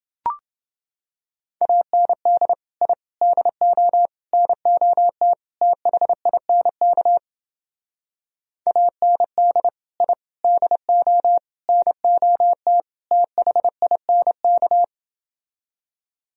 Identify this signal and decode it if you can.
Morse code, heard as a keyed tone, about 29 wpm, 700 Hz. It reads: AND I DO NOT THINK AND I DO NOT THINK